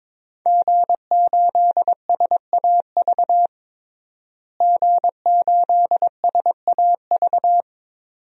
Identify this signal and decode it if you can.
Morse code, heard as a keyed tone, about 22 wpm, 700 Hz. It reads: G8SAV G8SAV
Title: DXCC-100 Call Signs - 5 Characters - Repeated 2x - 22wpm